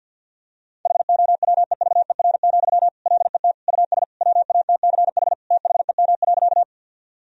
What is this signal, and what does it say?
Morse code, heard as a keyed tone, about 46 wpm, 700 Hz. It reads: HOWEVER, LET US WATCH THEM.
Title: Sentences from Top 1000 Words - Review Part 1 - 50wpm